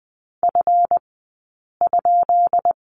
Morse code, heard as a keyed tone, about 20 wpm, spaced as for 8 wpm, 700 Hz. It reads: F?
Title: 177 Lesson 29d - Slash - Two Characters 20wpm